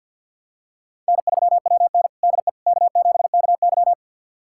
Morse code, heard as a keyed tone, about 48 wpm, 690 Hz. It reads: N3WN DE K7KY
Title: USA Callsign DE Callsign - 4 Characters - 50wpm